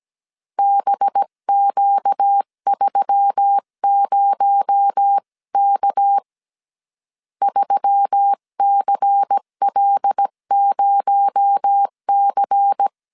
Morse code, heard as a keyed tone, about 17 wpm, 800 Hz. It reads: BQ30K 3CL0C
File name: CW / DET / 1 6 1500C S C 16wpm 2733